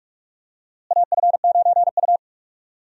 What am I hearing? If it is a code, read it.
Morse code, heard as a keyed tone, about 41 wpm, 700 Hz. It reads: AF9U
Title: USA Call Signs - 4 Characters - Repeated 2x - 45wpm